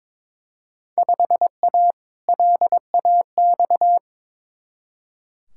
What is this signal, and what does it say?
Morse code, heard as a keyed tone, about 22 wpm, 700 Hz. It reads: 5A LAX
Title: Field Day Exchanges - 22wpm